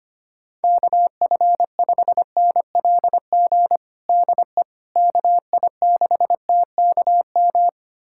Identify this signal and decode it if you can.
Morse code, heard as a keyed tone, about 25 wpm, 700 Hz. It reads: KF5NLG DE KI6TKM
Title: USA Callsign DE Callsign - 6 Characters - 25wpm